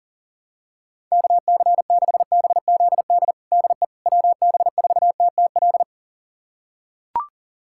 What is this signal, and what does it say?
Morse code, heard as a keyed tone, about 40 wpm, 700 Hz. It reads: KC6BZD DE WB4TTL
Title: USA Callsign DE Callsign - 6 Characters - 40wpm